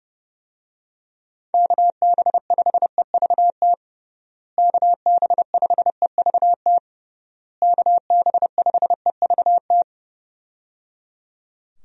Morse code, heard as a keyed tone, about 30 wpm, 700 Hz. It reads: KB5EVT KB5EVT KB5EVT
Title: USA Call Signs - 6 Characters - Repeated 3x - 30wpm